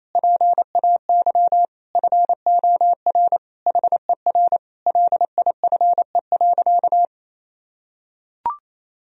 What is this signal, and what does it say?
Morse code, heard as a keyed tone, about 27 wpm, 700 Hz. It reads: PAY FOR HER LIFE.
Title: Sentences from Top 1000 Words - Review Part 4 - 28wpm